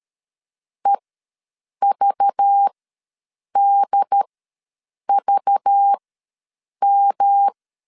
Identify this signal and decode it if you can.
Morse code, heard as a keyed tone, about 13 wpm, spaced as for 7 wpm, 800 Hz. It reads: EVDVM
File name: LEZIONI / 5 LBVC / 5 LBV ws 12wpm 3015